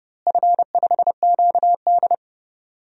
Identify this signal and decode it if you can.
Morse code, heard as a keyed tone, about 32 wpm, 700 Hz. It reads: F5QD
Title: DXCC-100 Call Signs - 4 Characters - Repeated 3x - 30wpm